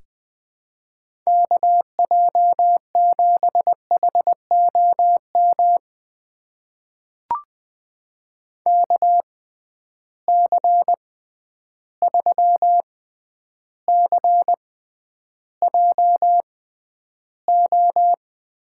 Morse code, heard as a keyed tone, about 20 wpm, spaced as for 10 wpm, 700 Hz. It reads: KJ7HOM KC3CJO
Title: USA Call Signs - 6 Characters - 6x Character Spacing - 20wpm